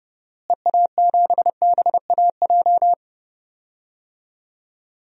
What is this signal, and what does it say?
Morse code, heard as a keyed tone, about 30 wpm, 700 Hz. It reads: EA7BAJ
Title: DXCC-100 Call Signs - 6 Characters - Repeated 3x - 30wpm